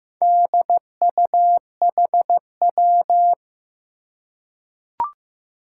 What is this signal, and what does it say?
Morse code, heard as a keyed tone, about 15 wpm, 700 Hz. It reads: DUHW
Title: Four Challenging Letter-Number Combinations 15wpm